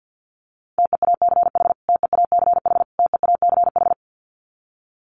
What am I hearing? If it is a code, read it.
Morse code, heard as a keyed tone, about 50 wpm, 700 Hz. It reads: TEACH TEACH TEACH
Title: Top 500 Words Repeated 50wpm